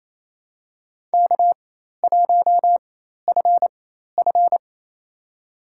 Morse code, heard as a keyed tone, about 28 wpm, spaced as for 13 wpm, 700 Hz. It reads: K1FF
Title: USA Call Signs - 4 Characters - 4x Character Spacing - 28wpm